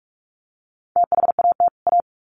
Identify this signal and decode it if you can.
Morse code, heard as a keyed tone, about 44 wpm, 700 Hz. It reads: THAT A
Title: Sets of 2 Words 45wpm